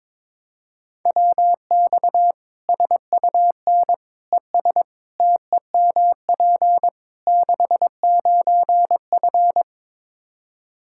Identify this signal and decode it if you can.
Morse code, heard as a keyed tone, about 22 wpm, 700 Hz. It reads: WX SUN ES TEMP 69F
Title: K1USN SST QSO Practice 22wpm